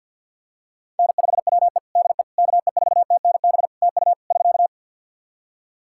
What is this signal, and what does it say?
Morse code, heard as a keyed tone, about 52 wpm, 700 Hz. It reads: N5WE DE KE4TNB TU <SK>
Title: K1USN SST QSO Practice 50wpm